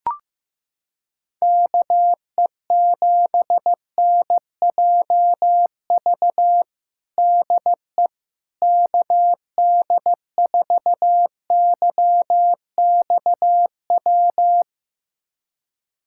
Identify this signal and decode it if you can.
Morse code, heard as a keyed tone, about 15 wpm, 700 Hz. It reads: KE7NJV DE KD4YXW
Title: USA Callsign DE Callsign 15wpm